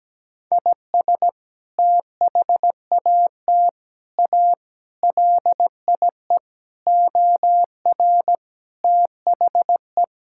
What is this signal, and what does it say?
Morse code, heard as a keyed tone, about 17 wpm, 700 Hz. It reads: IS THAT A LIE OR THE
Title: Sentences from Top 1600 Words 17wpm